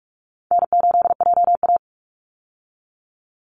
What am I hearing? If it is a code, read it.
Morse code, heard as a keyed tone, about 44 wpm, 700 Hz. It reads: N8JA
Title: Call Signs - 45wpm